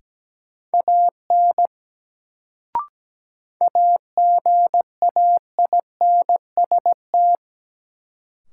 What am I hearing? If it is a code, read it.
Morse code, heard as a keyed tone, about 17 wpm, 700 Hz. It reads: AN AGAINST
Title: Sets of 2 Words - New Word Drill - 17wpm